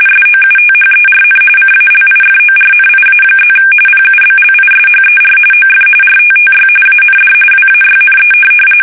RTTY, about 45.45 baud. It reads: SOUTHERN CALIFORNIA+ PUBLISHER OF RTTY AND WAYNE
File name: fsk850 / teletype-RTTY